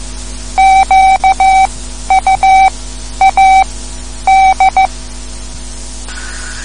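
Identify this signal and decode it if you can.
Morse code, heard as a keyed tone, about 15 wpm, spaced as for 10 wpm, 770 Hz. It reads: QUAD
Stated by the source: Morse code lesson, third lesson in a series for learning the Morse alphabet.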